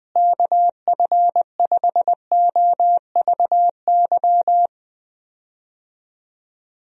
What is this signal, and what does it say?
Morse code, heard as a keyed tone, about 20 wpm, 700 Hz. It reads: KF5OVY
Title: USA Call Signs - 6 Characters - Repeated 2x - 20wpm